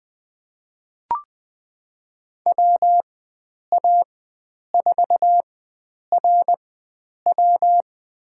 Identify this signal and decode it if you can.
Morse code, heard as a keyed tone, about 20 wpm, spaced as for 9 wpm, 700 Hz. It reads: WA4RW
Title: USA Call Signs - 5 Characters - 4x Character Spacing - 20wpm